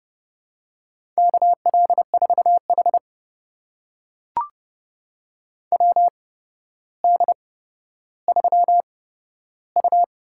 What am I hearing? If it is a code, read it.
Morse code, heard as a keyed tone, about 30 wpm, spaced as for 13 wpm, 700 Hz. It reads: KL4H WD3U
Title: USA Call Signs - 4 Characters - 8x Character Spacing - 30wpm